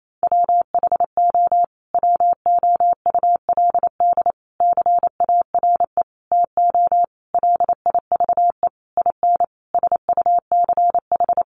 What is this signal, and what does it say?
Morse code, heard as a keyed tone, about 28 wpm, 700 Hz. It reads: WHO WOULD CARE TO LIVE IN SUCH
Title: Sentences from Top 1500 Words 28wpm